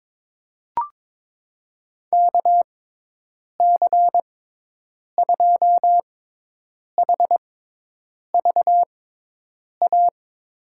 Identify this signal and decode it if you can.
Morse code, heard as a keyed tone, about 21 wpm, spaced as for 8 wpm, 700 Hz. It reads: KC2HVA
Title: USA Call Signs - 6 Characters - 6x Character Spacing - 22wpm